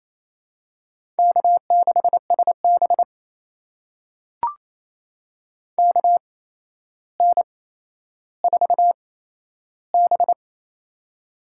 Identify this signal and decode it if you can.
Morse code, heard as a keyed tone, about 28 wpm, spaced as for 12 wpm, 700 Hz. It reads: K6SB KN4B
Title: USA Call Signs - 4 Characters - 8x Character Spacing - 28wpm